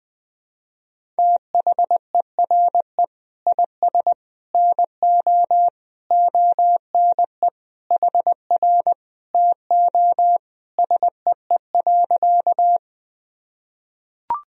Morse code, heard as a keyed tone, about 20 wpm, 700 Hz. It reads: THERE IS NO ONE HR TO SEE.
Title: Sentences from Top 200 Words and Abbreviations 20wpm